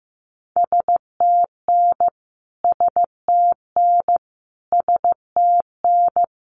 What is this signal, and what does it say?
Morse code, heard as a keyed tone, about 15 wpm, 700 Hz. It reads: STN STN STN
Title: Abbreviations Repeated 3x - 15wpm